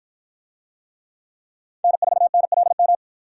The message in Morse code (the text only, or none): N4NLM